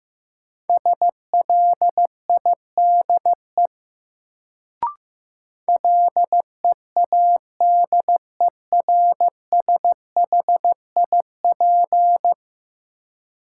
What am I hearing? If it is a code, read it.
Morse code, heard as a keyed tone, about 16 wpm, 700 Hz. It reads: SLIDE LEADERSHIP
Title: Sentences from Top 1600 Words - New Word Drill Alt - 15wpm